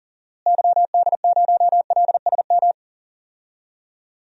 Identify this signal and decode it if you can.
Morse code, heard as a keyed tone, about 37 wpm, 700 Hz. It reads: YD0LSM
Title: DXCC-100 Call Signs - 6 Characters - Repeated 3x - 40wpm